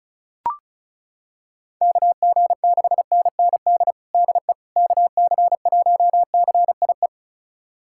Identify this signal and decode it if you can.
Morse code, heard as a keyed tone, about 36 wpm, 700 Hz. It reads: KG6NND DE KC1CIE
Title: USA Callsign DE Callsign - 6 Characters - 35wpm